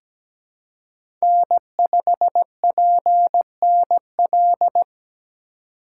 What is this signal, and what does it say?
Morse code, heard as a keyed tone, about 17 wpm, 700 Hz. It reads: N5PNL
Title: USA Call Signs - 5 Characters - 3x Character Spacing - 17wpm